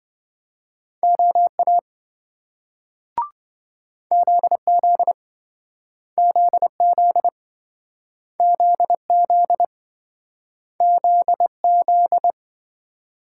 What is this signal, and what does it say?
Morse code, heard as a keyed tone, about 26 wpm, 700 Hz. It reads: OA ZZ ZZ ZZ ZZ